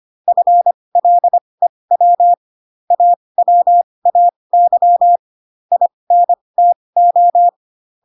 Morse code, heard as a keyed tone, about 25 wpm, spaced as for 20 wpm, 700 Hz. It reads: FLEW AWAY INTO